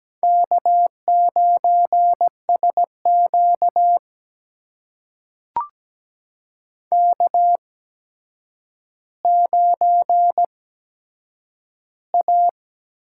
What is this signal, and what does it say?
Morse code, heard as a keyed tone, about 17 wpm, spaced as for 8 wpm, 700 Hz. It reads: K9SQ K9A
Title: USA Call Signs - 4 Characters - 8x Character Spacing - 17wpm